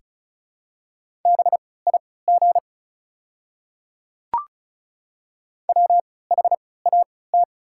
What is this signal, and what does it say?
Morse code, heard as a keyed tone, about 34 wpm, spaced as for 20 wpm, 690 Hz. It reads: BIG WHAT
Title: 3x Character Spacing - Top 300 Words - 35wpm